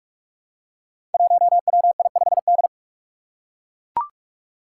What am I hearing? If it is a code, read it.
Morse code, heard as a keyed tone, about 44 wpm, 700 Hz. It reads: 1WI5D
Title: Five Challenging Letter-Number Combinations 45wpm